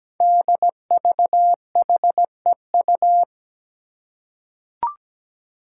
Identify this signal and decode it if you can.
Morse code, heard as a keyed tone, about 17 wpm, 700 Hz. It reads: DVHEU
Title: Five Challenging Letter-Number Combinations 17wpm